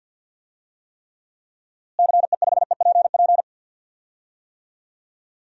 Morse code, heard as a keyed tone, about 53 wpm, 700 Hz. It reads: CE5EPP
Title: DXCC-100 Call Signs - 6 Characters - Repeated 3x - 50wpm